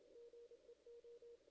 Morse code, heard as a keyed tone, about 31 wpm, 440 Hz. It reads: GEO
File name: 2016 JAN CW NAQP